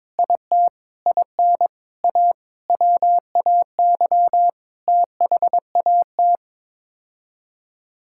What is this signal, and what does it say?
Morse code, heard as a keyed tone, about 22 wpm, 700 Hz. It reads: IT IN A WAY THAT
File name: practice / 22wpm / Sets of 7 Words - 22wpm